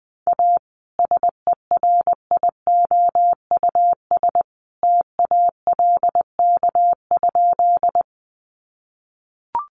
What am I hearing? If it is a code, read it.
Morse code, heard as a keyed tone, about 20 wpm, 700 Hz. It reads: A SERIOUS TALK?
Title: Sentences from Top 1100 Words 20wpm